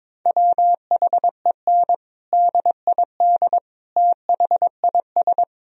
Morse code, heard as a keyed tone, about 22 wpm, 700 Hz. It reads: WHEN DID THIS
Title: Sentences from Top 1000 Words - Review Part 1 - 22wpm